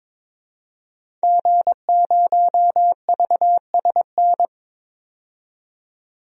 Morse code, heard as a keyed tone, about 22 wpm, 700 Hz. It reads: G0VSN
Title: DXCC-100 Call Signs - 5 Characters - 22wpm